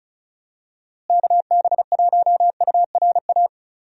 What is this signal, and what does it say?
Morse code, heard as a keyed tone, about 34 wpm, 700 Hz. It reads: KB1URA
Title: USA Call Signs - 6 Characters - Repeated 3x - 35wpm